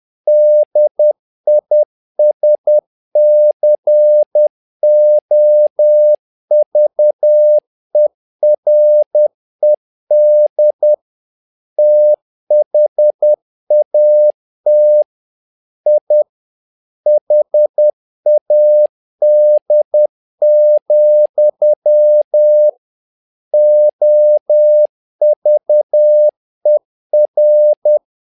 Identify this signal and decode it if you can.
Morse code, heard as a keyed tone, about 10 wpm, 600 Hz. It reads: DISCOVERED THAT I HAD, OVER
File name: learning center / tower trouble / tower trouble 10wpm